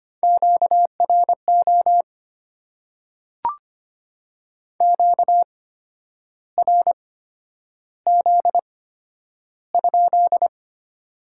Morse code, heard as a keyed tone, about 25 wpm, spaced as for 10 wpm, 700 Hz. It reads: QRO QRZ?
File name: Top 90 QSO Elements - 8x Character Spacing - 25wpm